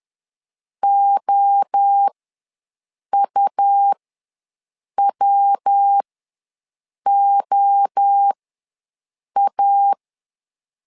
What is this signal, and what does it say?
Morse code, heard as a keyed tone, about 10 wpm, spaced as for 6 wpm, 800 Hz. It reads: OUWOA
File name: LEZIONI / 3 MWRO / 3 MWRO ws 10wpm 2310